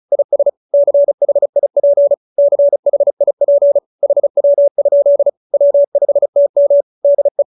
Morse code, heard as a keyed tone, about 35 wpm, 570 Hz. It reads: IS CHIP CHIP HW? W5TM DE